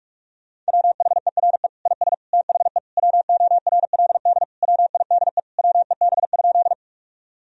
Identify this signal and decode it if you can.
Morse code, heard as a keyed tone, about 46 wpm, 700 Hz. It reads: WHERE IS THE WORLD WIDE WEB?
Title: Sentences from Top 1000 Words - Review Part 4 - 45wpm